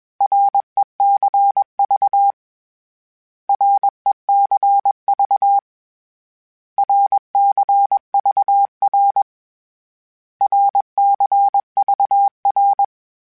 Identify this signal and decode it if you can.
Morse code, heard as a keyed tone, about 21 wpm, 800 Hz. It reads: RECV RECV RCVR RCVR
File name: Abreviations / Abreviations mp3s / Abreviations@20